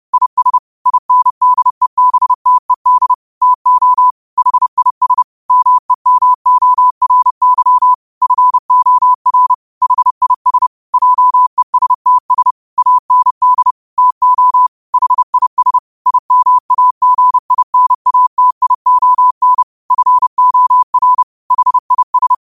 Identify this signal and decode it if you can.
Morse code, heard as a keyed tone, about 30 wpm, 1000 Hz. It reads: IS INDEBTED TO HIS MEMORY FOR HIS JESTS AND TO HIS IMAGINATION FOR HIS